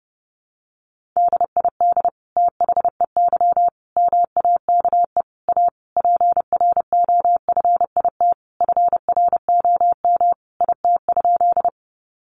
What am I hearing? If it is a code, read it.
Morse code, heard as a keyed tone, about 30 wpm, 700 Hz. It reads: DID THEY MAKE A PROFIT FROM IT?
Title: Sentences from Top 800 Words 30wpm